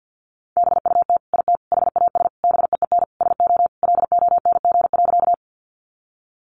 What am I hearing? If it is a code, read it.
Morse code, heard as a keyed tone, about 50 wpm, 700 Hz. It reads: BUT IT HAS BEEN SO LONG.
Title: Sentences from Top 100 Words 50wpm